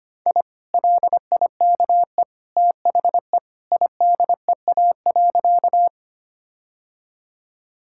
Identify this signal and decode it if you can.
Morse code, heard as a keyed tone, about 25 wpm, 700 Hz. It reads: I LIKE THE IDEA.
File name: practice / 25wpm / Sentences with 2-Word Phrases - 25wpm